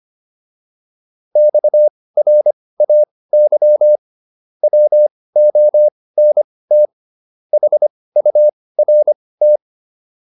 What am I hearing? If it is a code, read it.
Morse code, heard as a keyed tone, about 25 wpm, spaced as for 18 wpm, 600 Hz. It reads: XRAY WONT HURT
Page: intermediate-practice-files